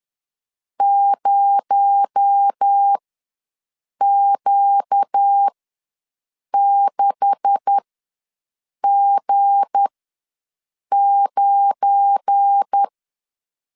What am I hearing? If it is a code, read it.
Morse code, heard as a keyed tone, about 10 wpm, spaced as for 6 wpm, 800 Hz. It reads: 0Q6G9
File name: LEZIONI / 9 286 ws 10wpm 4032